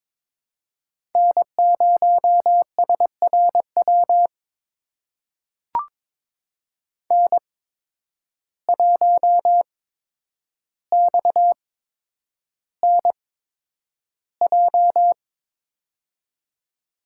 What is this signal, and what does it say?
Morse code, heard as a keyed tone, about 22 wpm, spaced as for 9 wpm, 700 Hz. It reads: N0SRW N1XNJ
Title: USA Call Signs - 5 Characters - 8x Character Spacing - 22wpm